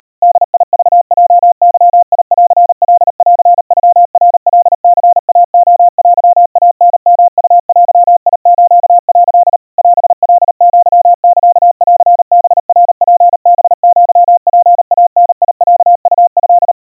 Morse code (Japanese Wabun code), heard as a keyed tone, about 36 wpm, 700 Hz. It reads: ボクヲゲンカンノナカニイレテイタヨウデス」 オカアサンハナツハアツイダロウト
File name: kawa / Morse sample / kamyu 140 cpm 700 Hz Crispy